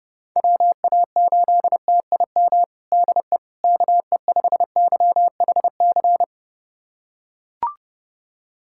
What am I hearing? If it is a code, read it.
Morse code, heard as a keyed tone, about 30 wpm, 700 Hz. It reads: WA8TIM DE KE5YHC
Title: USA Callsign DE Callsign 30wpm